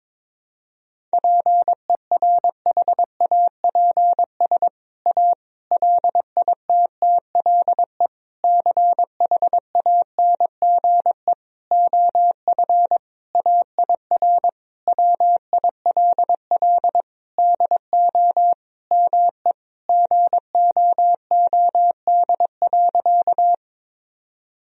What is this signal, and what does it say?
Morse code, heard as a keyed tone, about 22 wpm, 700 Hz. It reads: PERHAPS A LITTLE CHANGE OF AIR WILL DO ME GOOD.
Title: Sentences from Top 1000 Words - Review Part 1 - 22wpm